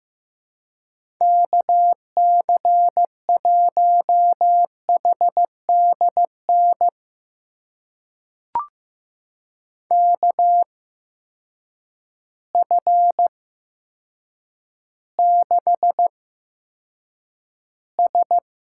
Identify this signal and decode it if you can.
Morse code, heard as a keyed tone, about 15 wpm, spaced as for 8 wpm, 700 Hz. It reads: KC1HDN KF6S
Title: USA Call Signs - 6 Characters - 8x Character Spacing - 15wpm